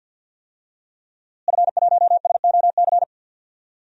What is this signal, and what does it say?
Morse code, heard as a keyed tone, about 52 wpm, 700 Hz. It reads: F1SOC
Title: DXCC-100 Call Signs - 5 Characters - Repeated 2x - 50wpm